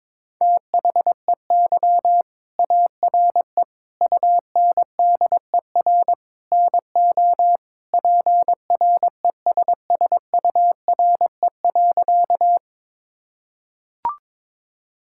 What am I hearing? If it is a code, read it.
Morse code, heard as a keyed tone, about 22 wpm, 700 Hz. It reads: THEY ARE UNDER NO PRESSURE.
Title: Sentences from Top 1000 Words - Review Part 6 - 22wpm